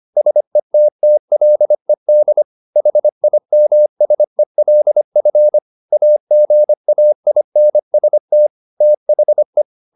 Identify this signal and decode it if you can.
Morse code, heard as a keyed tone, about 25 wpm, 600 Hz. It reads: SETTLED HIMSELF AGAINST THE
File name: the old man and the sea part 1 25wpm